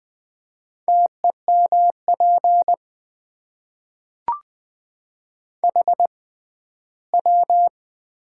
Morse code, heard as a keyed tone, about 20 wpm, spaced as for 14 wpm, 700 Hz. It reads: TEMP HW
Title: Top 90 QSO Elements - 6x Character Spacing - 20wpm